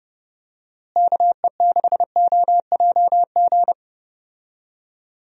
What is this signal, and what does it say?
Morse code, heard as a keyed tone, about 30 wpm, 700 Hz. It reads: KE6OJG
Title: USA Call Signs - 6 Characters - Repeated 3x - 30wpm